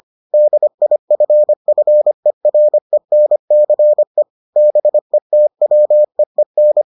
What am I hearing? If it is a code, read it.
Morse code, heard as a keyed tone, about 25 wpm, 600 Hz. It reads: DIFFERENCE BETWEEN